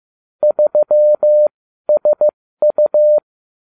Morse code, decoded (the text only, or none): 3SU